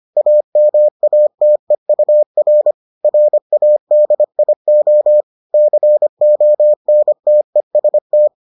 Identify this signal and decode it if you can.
Morse code, heard as a keyed tone, about 25 wpm, 600 Hz. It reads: AMATEUR RADIO CONTEST